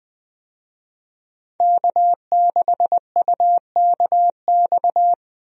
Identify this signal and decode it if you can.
Morse code, heard as a keyed tone, about 20 wpm, 700 Hz. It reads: K6UKX
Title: USA Call Signs - 5 Characters - 20wpm